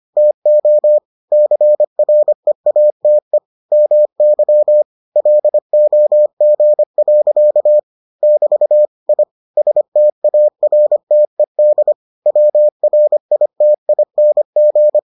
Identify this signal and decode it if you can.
Morse code, heard as a keyed tone, about 25 wpm, 600 Hz. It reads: TO CREATE MY LOG. <BT> I STARTED WRITING